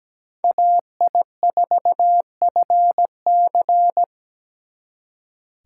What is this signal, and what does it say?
Morse code, heard as a keyed tone, about 20 wpm, 700 Hz. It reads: AI4FC
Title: Speed Racing - USA 2x2 Call Signs 17wpm